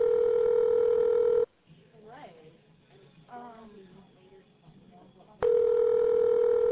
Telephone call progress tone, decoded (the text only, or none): ringback tone